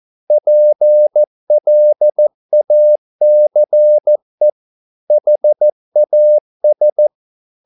Morse code, heard as a keyed tone, about 14 wpm, 600 Hz. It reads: PLACE HAS